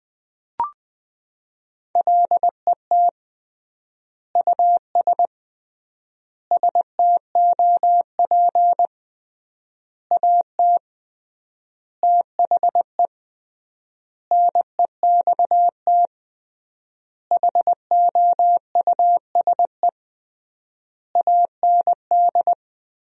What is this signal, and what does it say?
Morse code, heard as a keyed tone, about 21 wpm, 700 Hz. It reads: LET US STOP AT THE NEXT HOUSE AND
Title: 3x Word-Spacing - Sentences from Top 400 Words - 20wpm